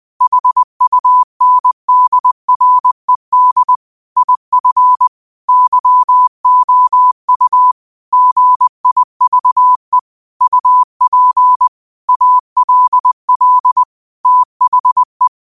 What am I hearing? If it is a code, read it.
Morse code, heard as a keyed tone, about 20 wpm, 1000 Hz. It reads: HUNDRED IF YOU GIVE UP ALL THE
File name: qotdfiles / 20wpm